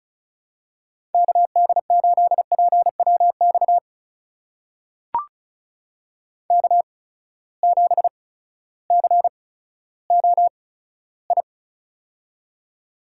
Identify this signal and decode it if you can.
Morse code, heard as a keyed tone, about 36 wpm, spaced as for 16 wpm, 700 Hz. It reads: KD8PWX K7COI
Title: USA Call Signs - 8x Character Spacing - 35wpm